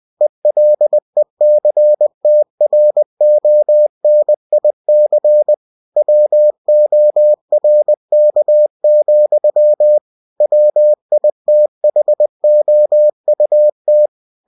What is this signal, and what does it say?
Morse code, heard as a keyed tone, about 20 wpm, 600 Hz. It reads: ELECTRONIC WORK, WITHOUT